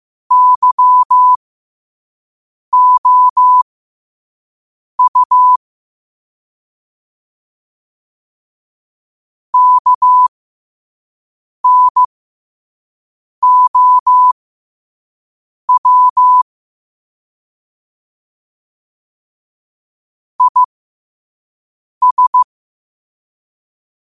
Morse code, heard as a keyed tone, about 15 wpm, spaced as for 5 wpm, 1000 Hz. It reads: YOU KNOW IS
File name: qotdfiles / 5wpm